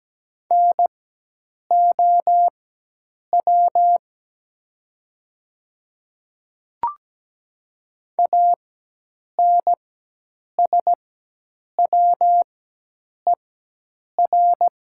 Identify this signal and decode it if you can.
Morse code, heard as a keyed tone, about 17 wpm, spaced as for 8 wpm, 700 Hz. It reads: NOW ANSWER
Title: 4x Character Spacing - Top 500 Words - 17wpm